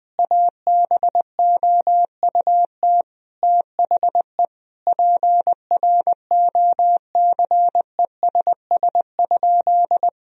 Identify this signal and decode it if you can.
Morse code, heard as a keyed tone, about 20 wpm, 700 Hz. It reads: ABOUT THE PROCESS?
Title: Sentences from Top 1000 Words - Review Part 3 - 20wpm